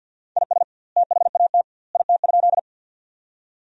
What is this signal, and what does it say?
Morse code, heard as a keyed tone, about 50 wpm, 700 Hz. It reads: IS THAT IT?
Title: 2x Word-Spacing - Sentences from Top 100 Words - 50wpm